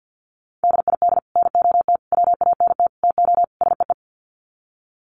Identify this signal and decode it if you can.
Morse code, heard as a keyed tone, about 49 wpm, 700 Hz. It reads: DID NOT WANT TO SEE
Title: Sets of 5 Words Encore2 50wpm